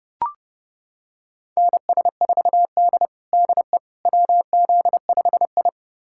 Morse code, heard as a keyed tone, about 30 wpm, 700 Hz. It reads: NS4D DE WZ5I
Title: USA Callsign DE Callsign - 4 Characters - 30wpm